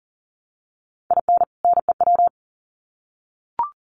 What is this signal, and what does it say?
Morse code, heard as a keyed tone, about 40 wpm, 700 Hz. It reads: IN NEW
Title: Sets of 2 Words 40wpm